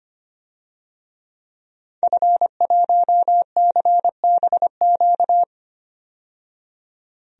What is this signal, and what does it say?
Morse code, heard as a keyed tone, about 25 wpm, 700 Hz. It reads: F1CBQ
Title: DXCC-100 Call Signs - 5 Characters - 25wpm